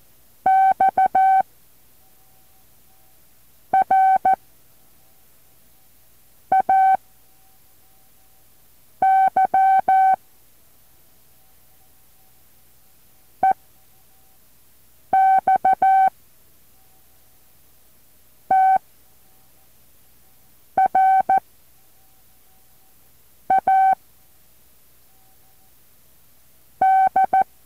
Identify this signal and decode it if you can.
Morse code, heard as a keyed tone, about 14 wpm, 760 Hz. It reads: X R A Y E X T R A D
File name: intl morse code2